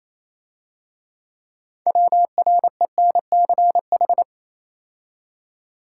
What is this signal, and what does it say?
Morse code, heard as a keyed tone, about 27 wpm, 700 Hz. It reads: WRENCH
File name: Easy Words - Speed-Racing with Hints - 22wpm